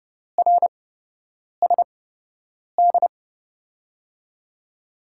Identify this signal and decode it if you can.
Morse code, heard as a keyed tone, about 29 wpm, spaced as for 8 wpm, 700 Hz. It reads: RSD
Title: USA Call Signs - 6 Characters - 8x Character Spacing - 30wpm